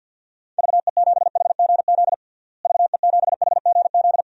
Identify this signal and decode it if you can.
Morse code, heard as a keyed tone, about 52 wpm, 700 Hz. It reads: VE7HGZ VE7HGZ
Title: DXCC-100 Call Signs - 6 Characters - Repeated 2x - 50wpm